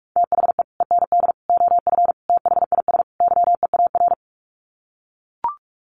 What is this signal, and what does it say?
Morse code, heard as a keyed tone, about 44 wpm, 700 Hz. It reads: THE END OF THIS YEAR
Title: Sets of 5 Words Encore 45wpm